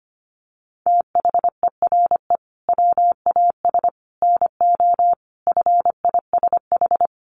Morse code, heard as a keyed tone, about 25 wpm, 700 Hz. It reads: THERE WAS NO FISH
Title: Sentences from Top 1500 Words 25wpm